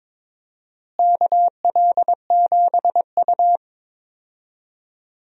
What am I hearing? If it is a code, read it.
Morse code, heard as a keyed tone, about 21 wpm, 700 Hz. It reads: KL7U
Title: DXCC-100 Call Signs - 4 Characters - Repeated 2x - 22wpm